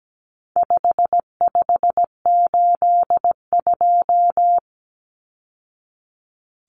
Morse code, heard as a keyed tone, about 17 wpm, 700 Hz. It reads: 5582
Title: Numbers - 4-Digit 17wpm